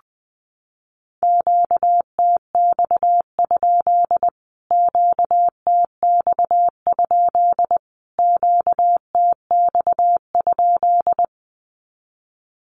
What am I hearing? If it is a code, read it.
Morse code, heard as a keyed tone, about 20 wpm, 700 Hz. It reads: QTX? QTX? QTX?
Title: Q-Codes Repeated 3x 20wpm